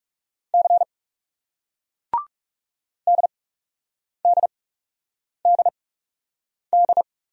Morse code, heard as a keyed tone, about 39 wpm, spaced as for 8 wpm, 700 Hz. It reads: C DDDD